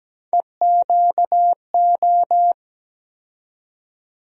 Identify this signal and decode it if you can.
Morse code, heard as a keyed tone, about 17 wpm, 700 Hz. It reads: EQO
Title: USA Call Sign Postfixes - 3 Letters 17wpm